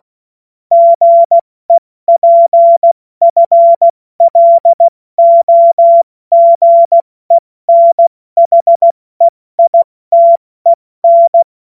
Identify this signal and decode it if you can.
Morse code, heard as a keyed tone, about 16 wpm, 680 Hz. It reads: GEPFLOGENHEITEN
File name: Morsekurs / Witze / Witz 25. Oktober 2022 70 F80